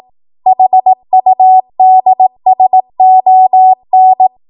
Morse code, heard as a keyed tone, about 18 wpm, 750 Hz. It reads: HUDSON
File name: Archive / 18 WPM / 18WPM